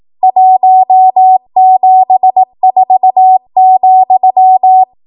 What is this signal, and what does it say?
Morse code, heard as a keyed tone, about 18 wpm, 750 Hz. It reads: 174,
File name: Archive / 18 WPM / 18WPM